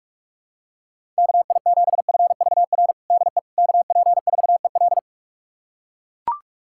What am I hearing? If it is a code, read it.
Morse code, heard as a keyed tone, about 42 wpm, 700 Hz. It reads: KI7FVR DE KP4EL
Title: USA Callsign DE Callsign 45wpm